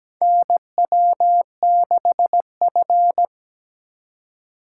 Morse code, heard as a keyed tone, about 17 wpm, 700 Hz. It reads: NW6F
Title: USA Call Signs - 4 Characters - 4x Character Spacing - 17wpm